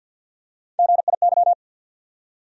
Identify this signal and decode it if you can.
Morse code, heard as a keyed tone, about 53 wpm, 700 Hz. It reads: GIY
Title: USA Call Sign Postfixes - 3 Letters 50wpm